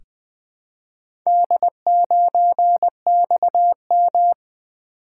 Morse code, heard as a keyed tone, about 20 wpm, 700 Hz. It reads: D9XM
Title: Four Letter-Number Combinations - 20wpm